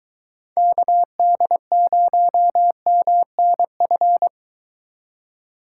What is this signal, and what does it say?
Morse code, heard as a keyed tone, about 23 wpm, 700 Hz. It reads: KD0MNF